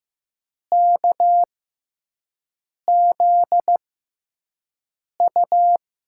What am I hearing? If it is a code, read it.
Morse code, heard as a keyed tone, about 15 wpm, spaced as for 5 wpm, 700 Hz. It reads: KZU
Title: USA Call Signs - 6x Character Spacing - 15wpm